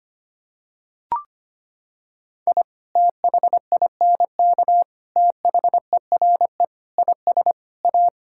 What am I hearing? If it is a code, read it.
Morse code, heard as a keyed tone, about 25 wpm, 700 Hz. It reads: I THINK THERE IS A
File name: practice / 25wpm / Sentences with 3-Word Phrases - 25wpm